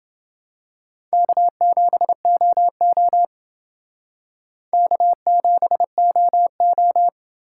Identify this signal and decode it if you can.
Morse code, heard as a keyed tone, about 29 wpm, 700 Hz. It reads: K7OO K7OO